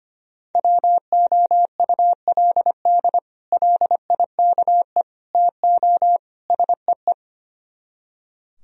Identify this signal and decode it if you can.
Morse code, heard as a keyed tone, about 33 wpm, 700 Hz. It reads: WOULD LIKE TO SEE